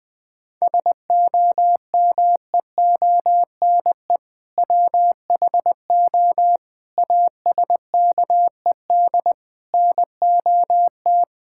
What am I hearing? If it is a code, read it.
Morse code, heard as a keyed tone, about 20 wpm, 700 Hz. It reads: SOMEONE WHO ASKED NOT
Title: Sentences with 7-Word Phrases - 20wpm